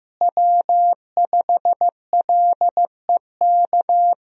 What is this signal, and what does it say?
Morse code, heard as a keyed tone, about 15 wpm, 700 Hz. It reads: W5LEK
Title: USA Call Signs - 5 Characters - Repeated 3x - 15wpm